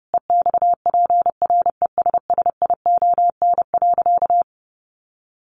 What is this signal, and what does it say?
Morse code, heard as a keyed tone, about 30 wpm, 700 Hz. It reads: EXPRESSION.
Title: Sentences from Top 1200 Words 30wpm